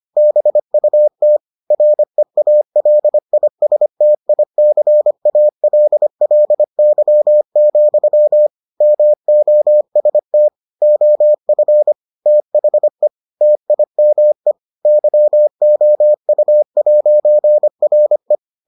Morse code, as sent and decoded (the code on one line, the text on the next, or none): -... ..- - / .-. .- .-.. .. ... - .. -.-. .- .-.. .-.. -.-- --..-- / -- --- ... - / --- ..-. / - .... / - .. -- . / -.-- --- ..- .----. .-. .
BUT REALISTICALLY, MOST OF THE TIME YOU'RE